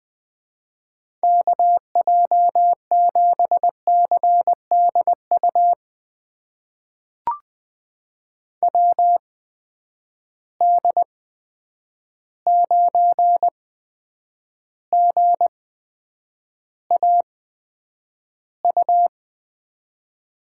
Morse code, heard as a keyed tone, about 20 wpm, spaced as for 9 wpm, 700 Hz. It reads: KJ7CDU WD9GAU
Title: USA Call Signs - 6 Characters - 8x Character Spacing - 20wpm